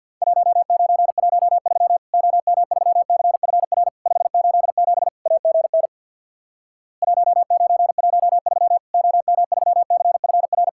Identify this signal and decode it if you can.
Morse code, heard as a keyed tone, about 50 wpm, 700 Hz. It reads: 1913 OM3CFR 587 AGN 1913 OM3CFR
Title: Worked All Europe DX Contest - QTC Exchange with Repeats - 50wpm